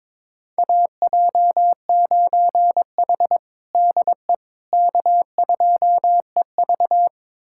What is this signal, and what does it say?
Morse code, heard as a keyed tone, about 22 wpm, 700 Hz. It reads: AJ9H DE K2EV
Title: USA Callsign DE Callsign - 4 Characters - 22wpm